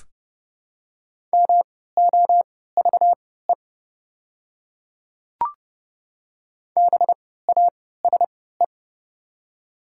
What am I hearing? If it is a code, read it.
Morse code, heard as a keyed tone, about 30 wpm, spaced as for 17 wpm, 700 Hz. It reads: MOVE BASE
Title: 3x Character Spacing - Top 300 Words - 30wpm